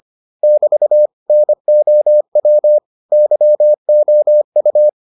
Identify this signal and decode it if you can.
Morse code, heard as a keyed tone, about 25 wpm, 600 Hz. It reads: <BT> NOW YOU